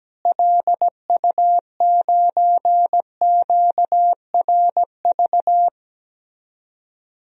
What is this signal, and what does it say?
Morse code, heard as a keyed tone, about 17 wpm, 700 Hz. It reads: LU9QRV
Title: DXCC-100 Call Signs - 6 Characters - Repeated 2x - 17wpm